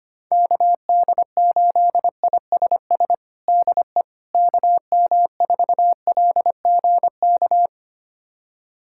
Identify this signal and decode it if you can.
Morse code, heard as a keyed tone, about 25 wpm, 700 Hz. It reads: KD8ISS DE KM4LGK
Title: USA Callsign DE Callsign - 6 Characters - 25wpm